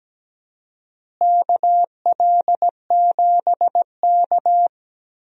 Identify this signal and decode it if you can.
Morse code, heard as a keyed tone, about 17 wpm, 700 Hz. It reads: KL7K
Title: DXCC-100 Call Signs - 4 Characters - Repeated 2x - 17wpm